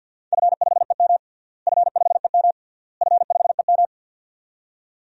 Morse code, heard as a keyed tone, about 52 wpm, 700 Hz. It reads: F5EM F5EM F5EM
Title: DXCC-100 Call Signs - 4 Characters - Repeated 3x - 50wpm